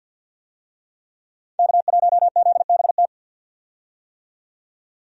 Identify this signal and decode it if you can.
Morse code, heard as a keyed tone, about 46 wpm, 700 Hz. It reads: K1ZBT
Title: USA Call Signs - 5 Characters - 50wpm